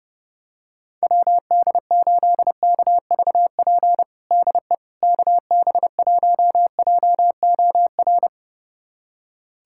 Morse code, heard as a keyed tone, about 31 wpm, 700 Hz. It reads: WD8KVP DE KB1JOR
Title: USA Callsign DE Callsign - 6 Characters - 30wpm